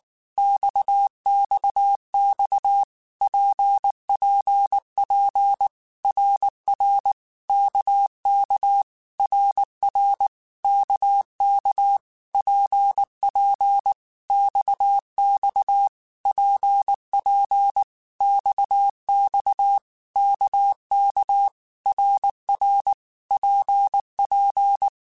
Morse code, heard as a keyed tone, about 19 wpm, 800 Hz. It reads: XXX PPP RR KK RR KK PP XX PP XX KK RR PP
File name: Basic Letter Groups By7Groups / RKPX / RKPX@18